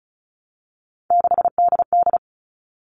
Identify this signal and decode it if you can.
Morse code, heard as a keyed tone, about 34 wpm, 700 Hz. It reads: 6DD
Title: All Cut Numbers - 3-Digit 35wpm